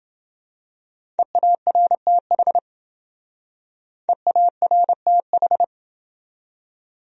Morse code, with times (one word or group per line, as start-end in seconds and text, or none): 1.2-2.6 EARTH
4.1-5.6 EARTH